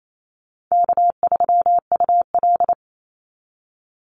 Morse code, heard as a keyed tone, about 28 wpm, 700 Hz. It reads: K3UL
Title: Easy Call Signs - 28wpm